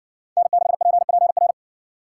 Morse code, heard as a keyed tone, about 60 wpm, 700 Hz. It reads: N6PPR